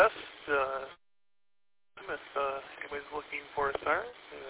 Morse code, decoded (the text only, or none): none